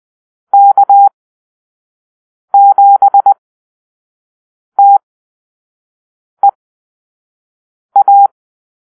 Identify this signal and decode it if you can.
Morse code, heard as a keyed tone, about 20 wpm, spaced as for 5 wpm, 800 Hz. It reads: K7TEA
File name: arrl / Call-Sign Drill-1